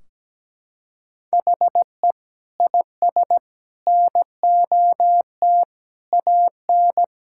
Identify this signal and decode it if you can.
Morse code, heard as a keyed tone, about 17 wpm, 700 Hz. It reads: HE IS NOT AN
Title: Sentences from Top 1600 Words 17wpm